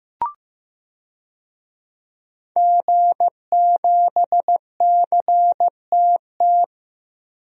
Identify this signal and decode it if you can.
Morse code, heard as a keyed tone, about 15 wpm, 700 Hz. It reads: G7CTT
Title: DXCC-100 Call Signs - 5 Characters - Repeated 3x - 15wpm